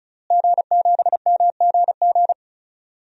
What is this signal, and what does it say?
Morse code, heard as a keyed tone, about 36 wpm, 700 Hz. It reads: G7MGG
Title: DXCC-100 Call Signs - 5 Characters - Repeated 2x - 35wpm